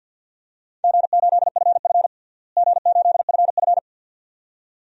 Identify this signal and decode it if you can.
Morse code, heard as a keyed tone, about 52 wpm, 700 Hz. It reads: G8FF G8FF
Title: DXCC-100 Call Signs - 4 Characters - Repeated 2x - 50wpm